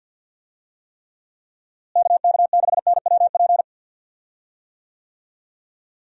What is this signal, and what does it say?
Morse code, heard as a keyed tone, about 53 wpm, 700 Hz. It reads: KK6NWP
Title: USA Call Signs - 6 Characters - Repeated 2x - 50wpm